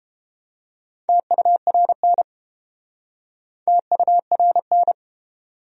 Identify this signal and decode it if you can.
Morse code, heard as a keyed tone, about 32 wpm, 700 Hz. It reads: TURN TURN